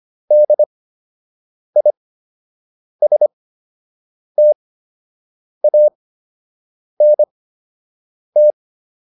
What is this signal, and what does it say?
Morse code, heard as a keyed tone, about 25 wpm, spaced as for 7 wpm, 600 Hz. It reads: DISTANT
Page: fundamental-practice-files